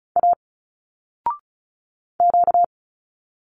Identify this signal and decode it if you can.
Morse code, heard as a keyed tone, about 34 wpm, 700 Hz. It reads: A Q
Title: Single Letters 35wpm